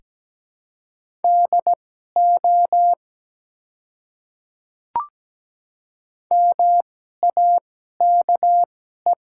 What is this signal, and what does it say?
Morse code, heard as a keyed tone, about 17 wpm, spaced as for 12 wpm, 700 Hz. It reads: DO MAKE
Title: 2x Character Spacing - Top 100 Words - 17wpm